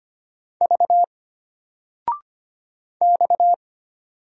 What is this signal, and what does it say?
Morse code, heard as a keyed tone, about 25 wpm, 700 Hz. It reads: V X
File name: Single-Letter-Number-25wpm